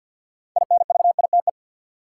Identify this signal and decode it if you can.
Morse code, heard as a keyed tone, about 52 wpm, 700 Hz. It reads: INVITE
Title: Six Letter Words - 50wpm